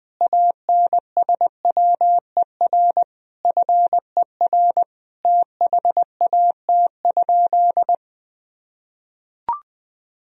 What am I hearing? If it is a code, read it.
Morse code, heard as a keyed tone, about 20 wpm, 700 Hz. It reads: ANSWER FER THAT?
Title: Sentences from Top 300 Words and Abbreviations 20wpm